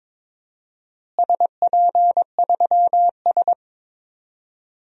Morse code, heard as a keyed tone, about 22 wpm, 700 Hz. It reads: SP3S
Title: DXCC-100 Call Signs - 4 Characters - Repeated 2x - 22wpm